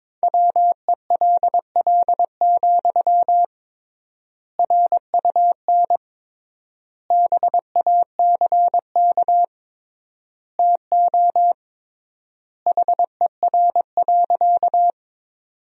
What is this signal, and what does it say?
Morse code, heard as a keyed tone, about 22 wpm, 700 Hz. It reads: WELL, RUN BACK TO HER.
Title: 3x Word-Spacing - Sentences from Top 300 Words - 22wpm